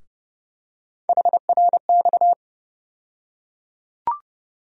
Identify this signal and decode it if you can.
Morse code, heard as a keyed tone, about 30 wpm, 700 Hz. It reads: HRX
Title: USA Call Sign Postfixes - 3 Letters 30wpm